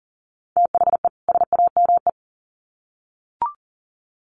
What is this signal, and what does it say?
Morse code, heard as a keyed tone, about 40 wpm, 700 Hz. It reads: THE SAME
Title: Sets of 2 Words 40wpm